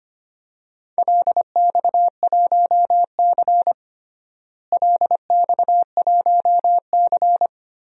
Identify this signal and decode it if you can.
Morse code, heard as a keyed tone, about 25 wpm, 700 Hz. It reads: LX1C LX1C
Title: DXCC-100 Call Signs - 4 Characters - Repeated 3x - 25wpm